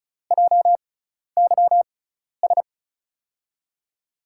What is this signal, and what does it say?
Morse code, heard as a keyed tone, about 36 wpm, spaced as for 12 wpm, 700 Hz. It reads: JYS
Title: USA Call Signs - 6 Characters - 6x Character Spacing - 35wpm